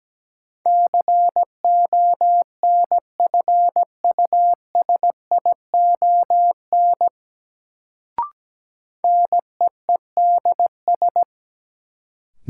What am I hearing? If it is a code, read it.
Morse code, heard as a keyed tone, about 17 wpm, 700 Hz. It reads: CONFUSION NEEDS
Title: Sentences from Top 1600 Words - New Word Drill Alt - 17wpm